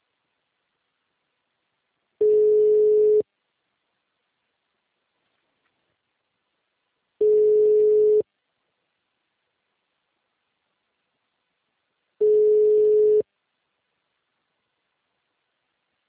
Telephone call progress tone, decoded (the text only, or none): ringback tone